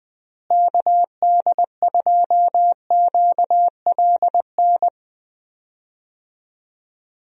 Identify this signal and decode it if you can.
Morse code, heard as a keyed tone, about 20 wpm, 700 Hz. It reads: KD2QLN
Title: USA Call Signs - 6 Characters - Repeated 2x - 20wpm